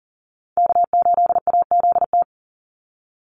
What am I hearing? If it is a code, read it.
Morse code, heard as a keyed tone, about 40 wpm, 700 Hz. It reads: K8AZT
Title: Call Signs Repeated 3x - 40wpm